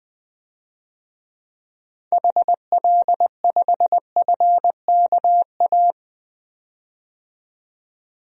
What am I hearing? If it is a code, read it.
Morse code, heard as a keyed tone, about 21 wpm, 700 Hz. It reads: HL5FKA
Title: DXCC-100 Call Signs - 6 Characters - 20wpm